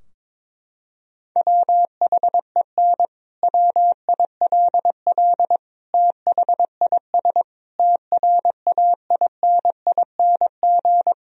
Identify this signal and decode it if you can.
Morse code, heard as a keyed tone, about 22 wpm, 700 Hz. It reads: WHEN WILL THIS TRAINING
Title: Sentences from Top 1000 Words - Review Part 4 - 22wpm